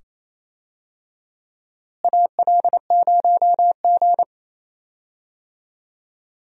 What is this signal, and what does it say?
Morse code, heard as a keyed tone, about 28 wpm, 700 Hz. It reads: AL0G
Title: USA Call Signs - 4 Characters - Repeated 2x - 28wpm